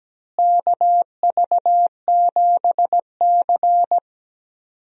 Morse code, heard as a keyed tone, about 17 wpm, 700 Hz. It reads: KV7C
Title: USA Call Signs - 4 Characters - Repeated 3x - 17wpm